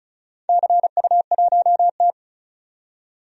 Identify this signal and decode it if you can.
Morse code, heard as a keyed tone, about 34 wpm, 700 Hz. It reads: CU1T
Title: DXCC-100 Call Signs - 4 Characters - 35wpm